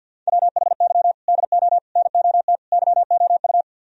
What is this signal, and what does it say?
Morse code, heard as a keyed tone, about 65 wpm, spaced as for 57 wpm, 690 Hz. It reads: WHY DO NOT YOU
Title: Speed-Racing - Sets of 4 Words 50wpm